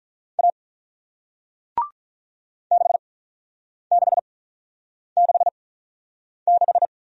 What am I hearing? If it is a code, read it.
Morse code, heard as a keyed tone, about 48 wpm, spaced as for 9 wpm, 700 Hz. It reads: A 6666